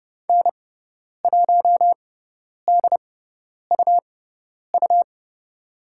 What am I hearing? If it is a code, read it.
Morse code, heard as a keyed tone, about 29 wpm, spaced as for 10 wpm, 700 Hz. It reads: N1DUU
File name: Lesson 17g - Letter u - Call Signs 30wpm